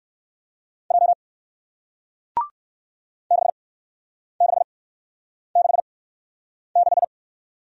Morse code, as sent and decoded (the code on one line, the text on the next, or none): .--- / -.... -.... -.... -....
J 6666